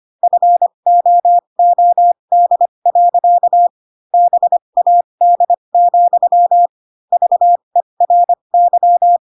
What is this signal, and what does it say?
Morse code, heard as a keyed tone, about 25 wpm, spaced as for 22 wpm, 700 Hz. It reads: FOOD. BAD, VERY
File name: Pinocchio / Pinocchio ch7 18 25 ik1hge